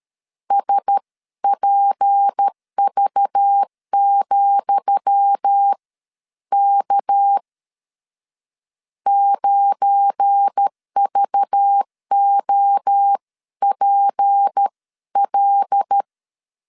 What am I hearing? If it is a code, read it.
Morse code, heard as a keyed tone, about 13 wpm, spaced as for 11 wpm, 800 Hz. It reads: SPV,K 9VOPL
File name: LEZIONI / 10 SIGNS / 10D COMA 12wpm 2544